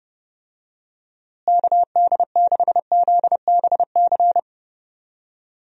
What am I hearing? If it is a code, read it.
Morse code, heard as a keyed tone, about 30 wpm, 700 Hz. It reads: KD6ZBC
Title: USA Call Signs - 6 Characters - Repeated 3x - 30wpm